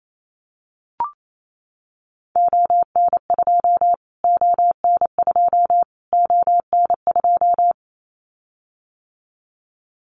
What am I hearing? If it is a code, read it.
Morse code, heard as a keyed tone, about 28 wpm, 700 Hz. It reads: ON2 ON2 ON2
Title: Callsign Prefixes for Top 25 DXCC - Repeated 3x - 28wpm